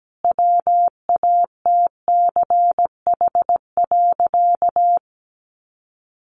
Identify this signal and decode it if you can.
Morse code, heard as a keyed tone, about 17 wpm, 700 Hz. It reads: WATCH.
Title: A Sound Of Thunder 17wpm